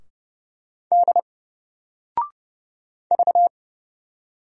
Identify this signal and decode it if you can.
Morse code, heard as a keyed tone, about 31 wpm, 700 Hz. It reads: D V
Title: Single Letter-Number 30wpm